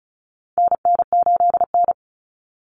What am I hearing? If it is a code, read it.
Morse code, heard as a keyed tone, about 34 wpm, 700 Hz. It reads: NN8N
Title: Common Cut Numbers - 4-Digit 35wpm